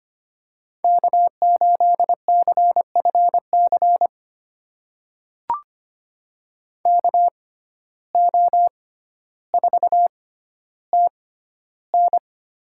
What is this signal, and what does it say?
Morse code, heard as a keyed tone, about 25 wpm, spaced as for 13 wpm, 700 Hz. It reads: K8CFC KO4TN
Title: USA Call Signs - 5 Characters - 6x Character Spacing - 25wpm